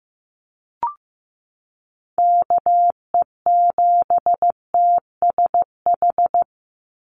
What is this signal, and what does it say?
Morse code, heard as a keyed tone, about 15 wpm, 700 Hz. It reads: KE7TSH
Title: USA Call Signs 15wpm